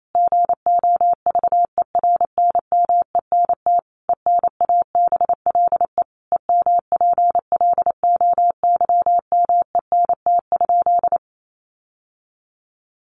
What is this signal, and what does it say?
Morse code, heard as a keyed tone, about 28 wpm, 700 Hz. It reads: GOVERNMENT ENABLE EMPLOYMENT?
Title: Sentences from Top 1500 Words 28wpm